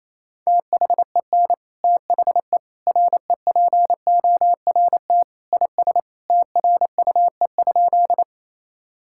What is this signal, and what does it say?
Morse code, heard as a keyed tone, about 27 wpm, 700 Hz. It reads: THEN THE REPORT IS TRUE?
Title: Sentences from Top 1000 Words - Review Part 6 - 28wpm